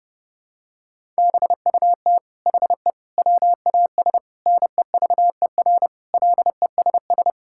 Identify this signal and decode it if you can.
Morse code, heard as a keyed tone, about 32 wpm, 700 Hz. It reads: BUT HE WAS NEVER LESS
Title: Sentences from Top 1000 Words - Review Part 5 - 30wpm